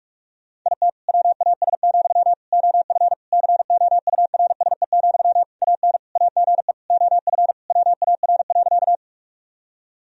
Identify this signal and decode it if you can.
Morse code, heard as a keyed tone, about 41 wpm, 700 Hz. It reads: IT WAS, OF COURSE, AN AGE OF WAR.
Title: Sentences from Top 1000 Words - Review Part 3 - 45wpm